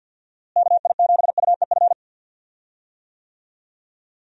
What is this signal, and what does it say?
Morse code, heard as a keyed tone, about 48 wpm, 700 Hz. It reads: KI7UEF
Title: USA Call Signs - 6 Characters - Repeated 3x - 50wpm